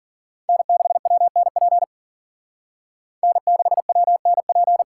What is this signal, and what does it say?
Morse code, heard as a keyed tone, about 51 wpm, 700 Hz. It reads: N6WNP N6WNP